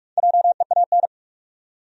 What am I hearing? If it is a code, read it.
Morse code, heard as a keyed tone, about 47 wpm, 700 Hz. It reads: JEAN
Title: USA Common Names - Max 4 Letters - 45wpm